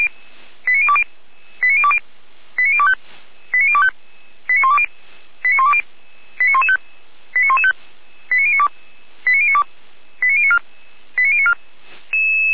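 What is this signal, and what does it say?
Five-tone selective call (ZVEI), heard as quick runs of five tones, tones of about 70 ms. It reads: 89020 89020 89025 89025 89120 89120 89115 89115 89092 89092 89094 89094
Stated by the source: ZVEI five-tone selective call received on an SDR.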